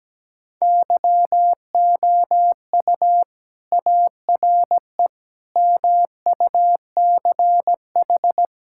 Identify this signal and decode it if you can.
Morse code, heard as a keyed tone, about 17 wpm, 700 Hz. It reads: YOU ARE MUCH
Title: Sentences from Top 1000 Words - Review Part 5 - 17wpm